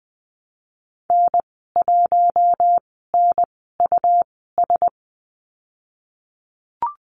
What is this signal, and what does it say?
Morse code, heard as a keyed tone, about 20 wpm, spaced as for 15 wpm, 700 Hz. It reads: N1NUS
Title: 193 Lesson 31g - Letter v - Call Signs 20wpm